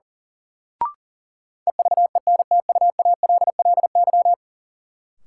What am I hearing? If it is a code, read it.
Morse code, heard as a keyed tone, about 37 wpm, 700 Hz. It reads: EVENTUALLY
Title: Sentences from Top 1000 Words - Review Part 6 - New Word Drill - 40wpm